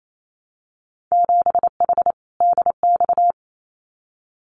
Morse code, heard as a keyed tone, about 28 wpm, 700 Hz. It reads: 7H DX
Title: Winter Field Day Exchanges Encore 28wpm